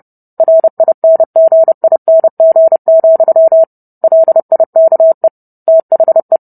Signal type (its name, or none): Morse code